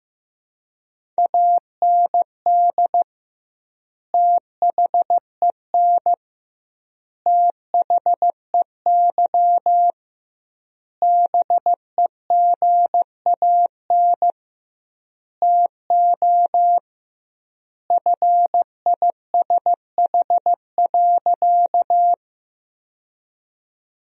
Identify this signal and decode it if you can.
Morse code, heard as a keyed tone, about 16 wpm, 700 Hz. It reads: AND THEN THEY BEGAN TO FISH.
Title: 2x Word-Spacing - Sentences from Top 300 Words - 15wpm